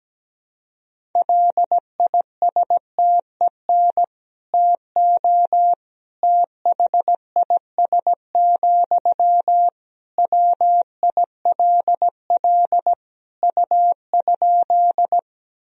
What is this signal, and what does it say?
Morse code, heard as a keyed tone, about 17 wpm, 700 Hz. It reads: LISTEN TO THIS, WILL U?
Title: Sentences from Top 400 Words and Abbreviations 17wpm